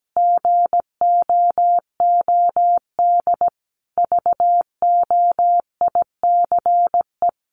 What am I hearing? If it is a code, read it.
Morse code, heard as a keyed tone, about 17 wpm, 700 Hz. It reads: GOOD VOICE
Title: Sentences from Top 500 Words 17wpm